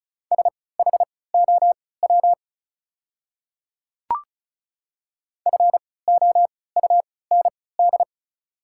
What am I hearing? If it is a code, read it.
Morse code, heard as a keyed tone, about 36 wpm, spaced as for 20 wpm, 690 Hz. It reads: SHOW FOUND
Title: 3x Character Spacing - Top 300 Words - 35wpm